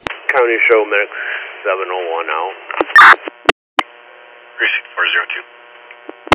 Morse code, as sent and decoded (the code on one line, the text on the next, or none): none